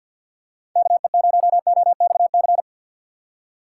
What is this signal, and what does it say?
Morse code, heard as a keyed tone, about 53 wpm, 700 Hz. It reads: KE0OXC
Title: USA Call Signs - 6 Characters - 4x Character Spacing - 50wpm